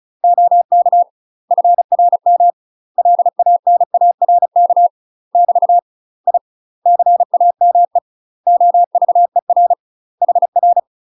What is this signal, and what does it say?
Morse code, heard as a keyed tone, about 34 wpm, 700 Hz. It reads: OK FRM LANARK <BT> I CAME OVER HR
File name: UK MASTER QSOs TRAINING / 35WPM / UK MASTER QSOs 0034